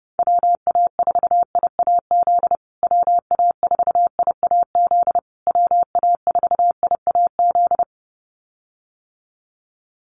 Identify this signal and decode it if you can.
Morse code, heard as a keyed tone, about 30 wpm, 700 Hz. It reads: WA4IAZ WA4IAZ WA4IAZ
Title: USA Call Signs Repeated 3x 30wpm